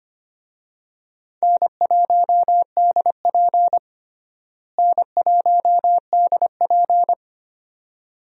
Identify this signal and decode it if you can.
Morse code, heard as a keyed tone, about 25 wpm, 700 Hz. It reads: N1DP N1DP
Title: USA Call Signs - 4 Characters - Repeated 2x - 25wpm